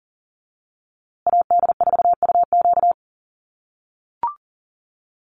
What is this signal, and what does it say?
Morse code, heard as a keyed tone, about 40 wpm, 700 Hz. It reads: AD4UQ
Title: USA Call Signs Repeated 3x 40wpm